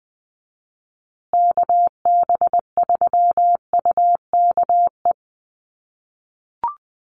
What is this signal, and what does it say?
Morse code, heard as a keyed tone, about 20 wpm, 700 Hz. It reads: KB3UKE
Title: USA Call Signs Repeated 2x 20wpm